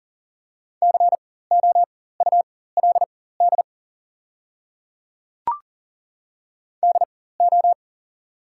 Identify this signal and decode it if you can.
Morse code, heard as a keyed tone, about 40 wpm, spaced as for 19 wpm, 700 Hz. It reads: COULD DO
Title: 4x Character Spacing - Top 300 Words - 40wpm